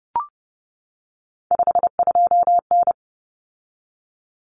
Morse code, heard as a keyed tone, about 30 wpm, 700 Hz. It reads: 52N
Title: Common Cut Numbers - 3-Digit 30wpm